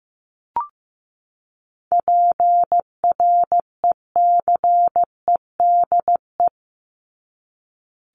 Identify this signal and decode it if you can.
Morse code, heard as a keyed tone, about 15 wpm, 700 Hz. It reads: PRECEDE
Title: ING Suffix - Top 100 Words with Base Words 15wpm